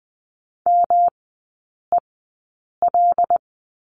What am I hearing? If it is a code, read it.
Morse code, heard as a keyed tone, about 20 wpm, spaced as for 8 wpm, 700 Hz. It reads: MEL
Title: 121 Lesson 20e - Number 3 - Three Characters 20wpm